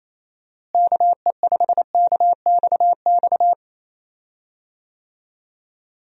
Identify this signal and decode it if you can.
Morse code, heard as a keyed tone, about 27 wpm, 700 Hz. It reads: KE5KXX
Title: USA Call Signs - 6 Characters - Repeated 2x - 28wpm